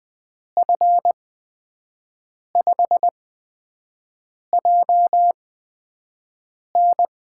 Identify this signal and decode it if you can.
Morse code, heard as a keyed tone, about 20 wpm, spaced as for 5 wpm, 700 Hz. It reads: F5JN
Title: USA Call Signs - 5 Characters - 8x Character Spacing - 20wpm